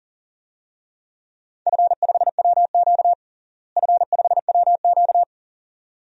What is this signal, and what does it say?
Morse code, heard as a keyed tone, about 40 wpm, 700 Hz. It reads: F5WQ F5WQ
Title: DXCC-100 Call Signs - 4 Characters - Repeated 2x - 40wpm